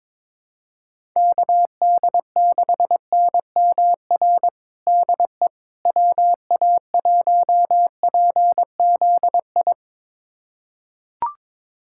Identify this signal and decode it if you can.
Morse code, heard as a keyed tone, about 22 wpm, 700 Hz. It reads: KD6NMR DE WA1PZI
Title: USA Callsign DE Callsign 22wpm